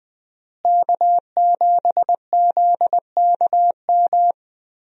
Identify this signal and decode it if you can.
Morse code, heard as a keyed tone, about 21 wpm, 700 Hz. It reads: K7ZKM
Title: USA Call Signs - 5 Characters - Repeated 2x - 20wpm